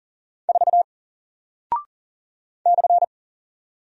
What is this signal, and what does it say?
Morse code, heard as a keyed tone, about 38 wpm, 700 Hz. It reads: 4 /
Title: Single Character (Letter:Number:Punctuation:Prosign) Alt 40wpm